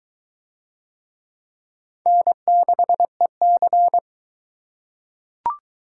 Morse code, heard as a keyed tone, about 20 wpm, 700 Hz. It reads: N6EC